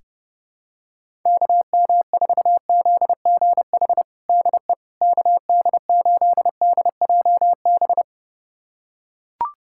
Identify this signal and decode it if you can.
Morse code, heard as a keyed tone, about 30 wpm, 700 Hz. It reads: KM4ZGH DE KD8DJB
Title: USA Callsign DE Callsign 30wpm